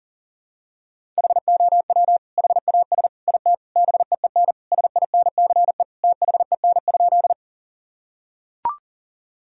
Morse code, heard as a keyed tone, about 40 wpm, 700 Hz. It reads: HOW HAS IT BEEN SINCE THEN?
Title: Sentences from Top 1000 Words - Review Part 1 - 40wpm